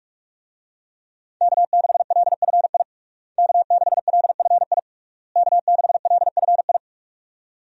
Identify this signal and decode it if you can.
Morse code, heard as a keyed tone, about 41 wpm, 700 Hz. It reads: K6LFI K6LFI K6LFI
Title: USA Call Signs - 5 Characters - Repeated 3x - 45wpm